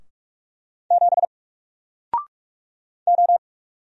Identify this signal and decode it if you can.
Morse code, heard as a keyed tone, about 44 wpm, 700 Hz. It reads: 7 O
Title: Single Letter-Number Alt 45wpm